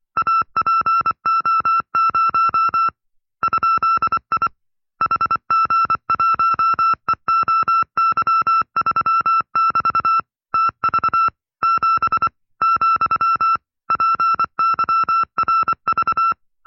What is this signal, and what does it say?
Morse code, heard as a keyed tone, about 24 wpm, 1340 Hz. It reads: APO0 ?I HG1EOY3<BT> TV 7 , PYRV
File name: FreeFiles / CW-practice / 20WPM-side2